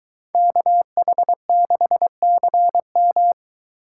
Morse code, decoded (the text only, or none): KH6CM